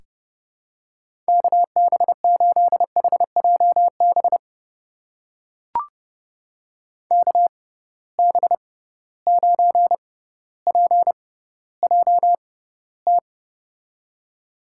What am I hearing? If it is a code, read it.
Morse code, heard as a keyed tone, about 32 wpm, spaced as for 16 wpm, 700 Hz. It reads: KB8HJB KB9PJT
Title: USA Call Signs - 6 Characters - 6x Character Spacing - 30wpm